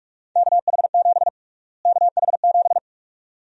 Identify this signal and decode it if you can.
Morse code, heard as a keyed tone, about 46 wpm, 700 Hz. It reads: KH7 KH7
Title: USA Call Sign Prefixes - Repeated 2x - 45wpm